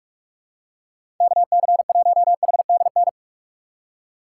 Morse code, heard as a keyed tone, about 44 wpm, 700 Hz. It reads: KC1HDN
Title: USA Call Signs - 6 Characters - 6x Character Spacing - 45wpm